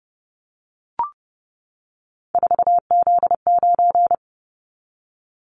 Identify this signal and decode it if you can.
Morse code, heard as a keyed tone, about 30 wpm, 700 Hz. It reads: 4Z9
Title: Callsign Prefixes for Top 50 DXCC - 30wpm